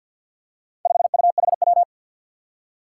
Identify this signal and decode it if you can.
Morse code, heard as a keyed tone, about 53 wpm, 700 Hz. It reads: 5UHW
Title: Four Challenging Letter-Number Combinations 50wpm